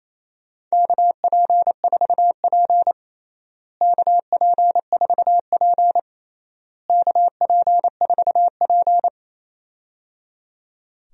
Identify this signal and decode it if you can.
Morse code, heard as a keyed tone, about 28 wpm, 700 Hz. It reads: KP4P KP4P KP4P
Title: DXCC-100 Call Signs - 4 Characters - Repeated 3x - 28wpm